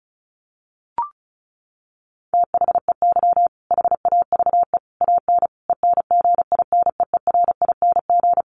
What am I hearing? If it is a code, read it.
Morse code, heard as a keyed tone, about 34 wpm, 700 Hz. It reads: THEY HAVE AN ENGINEERING
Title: Sentences from Top 1500 Words 35wpm